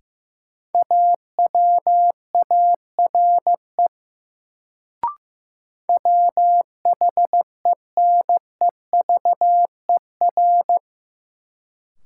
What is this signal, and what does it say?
Morse code, heard as a keyed tone, about 15 wpm, 700 Hz. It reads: AWARE WHENEVER
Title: Sentences from Top 1200 Words - New Word Drill - 15wpm